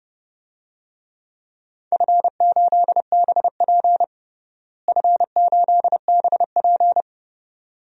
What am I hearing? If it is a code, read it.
Morse code, heard as a keyed tone, about 30 wpm, 700 Hz. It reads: F8BP F8BP
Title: DXCC-100 Call Signs - 4 Characters - Repeated 2x - 30wpm